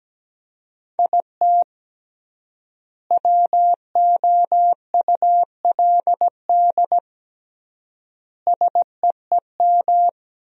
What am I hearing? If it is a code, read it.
Morse code, heard as a keyed tone, about 17 wpm, 700 Hz. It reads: IT WOULD SEEM
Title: 3x Word-Spacing - Sentences from Top 300 Words - 17wpm